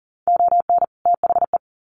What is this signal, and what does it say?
Morse code, heard as a keyed tone, about 38 wpm, 700 Hz. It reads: ON THE
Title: Sets of 2 Words 40wpm